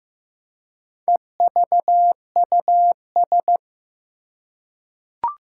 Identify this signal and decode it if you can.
Morse code, heard as a keyed tone, about 15 wpm, 700 Hz. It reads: EVUS
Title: Four Challenging Letter-Number Combinations 15wpm